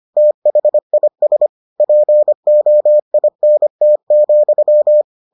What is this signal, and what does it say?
Morse code, heard as a keyed tone, about 25 wpm, 600 Hz. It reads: THIS POINT,